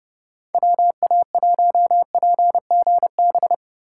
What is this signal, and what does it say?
Morse code, heard as a keyed tone, about 39 wpm, 700 Hz. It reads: WA1PGB